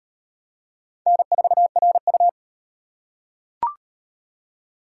Morse code, heard as a keyed tone, about 38 wpm, 700 Hz. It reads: N4RU